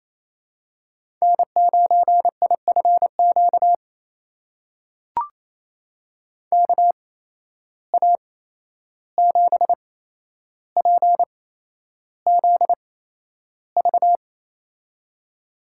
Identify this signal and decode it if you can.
Morse code, heard as a keyed tone, about 28 wpm, spaced as for 11 wpm, 700 Hz. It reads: N9IFQ KA7PZV
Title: USA Call Signs - 8x Character Spacing - 28wpm